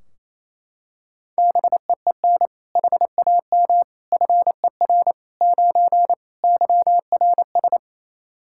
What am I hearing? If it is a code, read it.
Morse code, heard as a keyed tone, about 27 wpm, 700 Hz. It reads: BEEN HAM FER 9 YRS
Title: Practice QSO - Exchange 3 - More Info - 28wpm